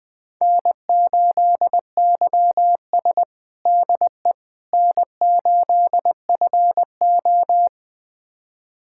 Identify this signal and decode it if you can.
Morse code, heard as a keyed tone, about 20 wpm, 700 Hz. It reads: N8YS DE N8FO
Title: USA Callsign DE Callsign - 4 Characters - 20wpm